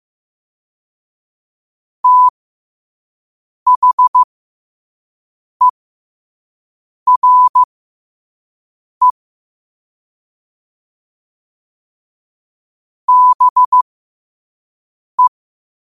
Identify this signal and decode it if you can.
Morse code, heard as a keyed tone, about 15 wpm, spaced as for 5 wpm, 1000 Hz. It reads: THERE BE
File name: qotdfiles / 5wpm